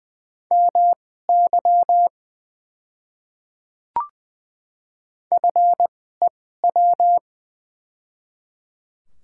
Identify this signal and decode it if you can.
Morse code, heard as a keyed tone, about 20 wpm, spaced as for 14 wpm, 700 Hz. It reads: MY FEW
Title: 2x Character Spacing - Top 300 Words - 20wpm